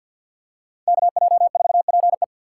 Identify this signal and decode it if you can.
Morse code, heard as a keyed tone, about 46 wpm, 700 Hz. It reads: KJ4PE
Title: USA Call Signs - 5 Characters - 2x Character Spacing - 50wpm